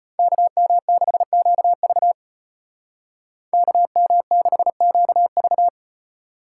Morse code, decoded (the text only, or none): KM6QV KM6QV